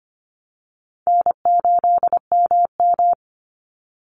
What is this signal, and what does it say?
Morse code, heard as a keyed tone, about 25 wpm, 700 Hz. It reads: N8MM
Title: USA Call Signs 25wpm